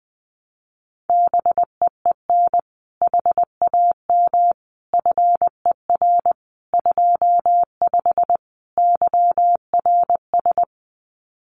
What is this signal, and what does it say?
Morse code, heard as a keyed tone, about 20 wpm, 700 Hz. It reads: BEEN HAM FER 25 YRS
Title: Common QSO Phrases Encore 20wpm